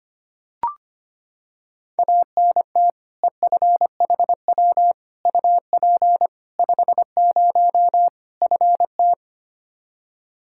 Morse code, heard as a keyed tone, about 25 wpm, 700 Hz. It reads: ANT EFHW UP 50 FT
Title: Practice QSO - Exchange 2 - Info - 25wpm